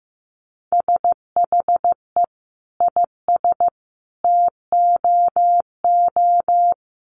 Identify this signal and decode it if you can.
Morse code, heard as a keyed tone, about 15 wpm, 700 Hz. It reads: SHE IS TOO
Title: Sentences from Top 1300 Words 15wpm